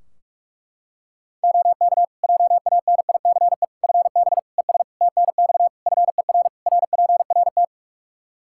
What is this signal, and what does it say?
Morse code, heard as a keyed tone, about 41 wpm, 700 Hz. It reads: OK JANICE FB ES TNX FER RPRT
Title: K1USN SST QSO Practice 45wpm